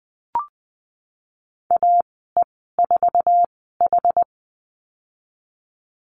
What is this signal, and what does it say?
Morse code, heard as a keyed tone, about 20 wpm, spaced as for 15 wpm, 700 Hz. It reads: AE4H
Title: 088 Lesson 15g - Number 5 - Call Signs 20wpm